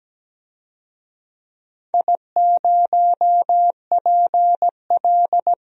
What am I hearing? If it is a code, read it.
Morse code, heard as a keyed tone, about 17 wpm, 700 Hz. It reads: I0PL
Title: DXCC-100 Call Signs - 4 Characters - Repeated 3x - 17wpm